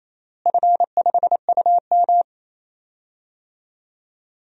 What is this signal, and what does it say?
Morse code, heard as a keyed tone, about 28 wpm, 700 Hz. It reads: F5UM
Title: DXCC-100 Call Signs - 4 Characters - Repeated 2x - 28wpm